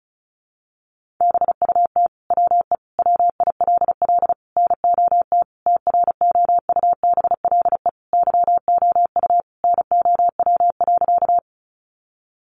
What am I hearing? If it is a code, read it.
Morse code, heard as a keyed tone, about 34 wpm, 700 Hz. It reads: BUT WE WILL NOT TROUBLE YOU NOW.
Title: Sentences from Top 1200 Words 35wpm